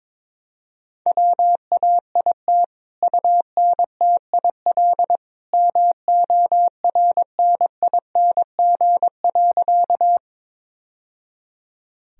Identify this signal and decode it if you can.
Morse code, heard as a keyed tone, about 21 wpm, 700 Hz. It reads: WAIT UNTIL MORNING.
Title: Sentences from Top 1000 Words - Review Part 1 - 22wpm